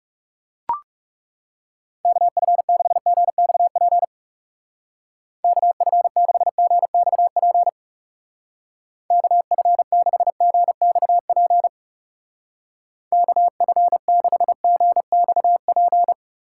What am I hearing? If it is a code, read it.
Morse code, heard as a keyed tone, about 38 wpm, 700 Hz. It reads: KF6GXP KF6GXP KF6GXP KF6GXP